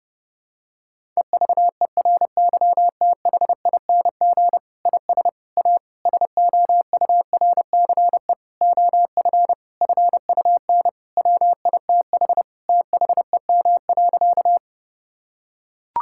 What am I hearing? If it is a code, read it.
Morse code, heard as a keyed tone, about 30 wpm, 700 Hz. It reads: EVERYTHING IS A SOURCE OF FUN WITH THEM.
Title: Sentences from Top 1000 Words - Review Part 5 - 30wpm